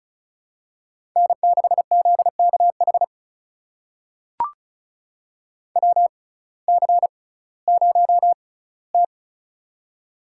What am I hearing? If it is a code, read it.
Morse code, heard as a keyed tone, about 34 wpm, spaced as for 19 wpm, 700 Hz. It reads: N6ZKH WC0T
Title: USA Call Signs - 6x Character Spacing - 35wpm